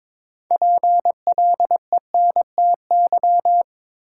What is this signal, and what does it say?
Morse code, heard as a keyed tone, about 21 wpm, 700 Hz. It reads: PLENTY
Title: Sentences from Top 1300 Words - New Word Drill Alt - 22wpm